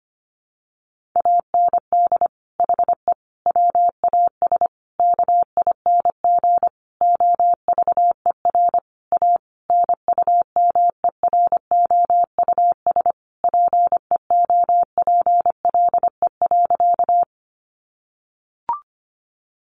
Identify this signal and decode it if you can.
Morse code, heard as a keyed tone, about 25 wpm, 700 Hz. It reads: AND HE WAS KING OVER A NUMEROUS PEOPLE.
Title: Sentences from Top 1400 Words 25wpm